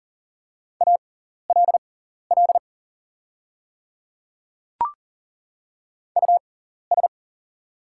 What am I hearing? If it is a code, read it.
Morse code, heard as a keyed tone, about 39 wpm, spaced as for 14 wpm, 700 Hz. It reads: ALL US
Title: 6x Character Spacing - Top 200 Words - 40wpm